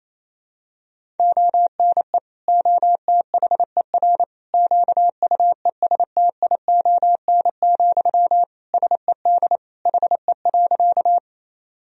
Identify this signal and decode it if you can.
Morse code, heard as a keyed tone, about 27 wpm, 700 Hz. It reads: ONE OTHER QUESTION, SED HE.
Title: Sentences from Top 400 Words and Abbreviations 28wpm